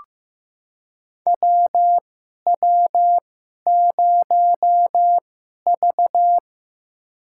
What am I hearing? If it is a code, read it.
Morse code, heard as a keyed tone, about 15 wpm, spaced as for 11 wpm, 700 Hz. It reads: WW0V
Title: USA Call Signs - 2x Character Spacing - 15wpm